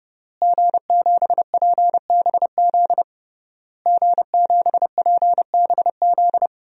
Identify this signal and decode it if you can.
Morse code, heard as a keyed tone, about 30 wpm, 700 Hz. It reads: G7PBZ G7PBZ
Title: DXCC-100 Call Signs - 5 Characters - Repeated 3x - 30wpm